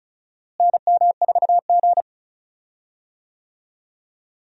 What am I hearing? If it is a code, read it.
Morse code, heard as a keyed tone, about 36 wpm, 700 Hz. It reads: NM4G
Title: USA Call Signs - 4 Characters - Repeated 2x - 35wpm